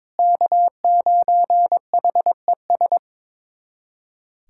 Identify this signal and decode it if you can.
Morse code, heard as a keyed tone, about 21 wpm, 700 Hz. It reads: K9HES
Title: USA Callsign DE Callsign - 5 Characters - 22wpm